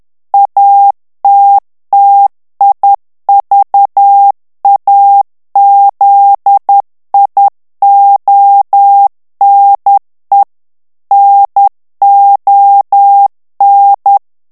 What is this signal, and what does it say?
Morse code, heard as a keyed tone, about 10 wpm, 800 Hz. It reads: ATTIVAZIONE NON
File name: morse mp3 / ITALIEN texte clair 1 vitesse 10 mots minute